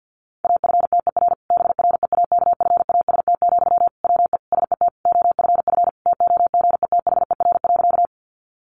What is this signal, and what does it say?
Morse code, heard as a keyed tone, about 50 wpm, 700 Hz. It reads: AFTER BREAKFAST, WE SET OFF TOGETHER.
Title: Sentences from Top 1500 Words 50wpm